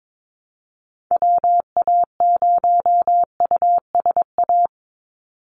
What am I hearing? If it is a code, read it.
Morse code, heard as a keyed tone, about 22 wpm, 700 Hz. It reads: WA0USA
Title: Call Signs Repeated 3x - 22wpm